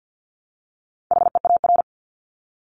Morse code, heard as a keyed tone, about 50 wpm, 700 Hz. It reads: HEAR
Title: Top 300 Words Repeated 50wpm